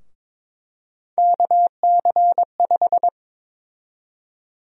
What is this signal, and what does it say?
Morse code, heard as a keyed tone, about 22 wpm, 700 Hz. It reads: KC5
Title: USA Call Sign Prefixes - Repeated 2x - 22wpm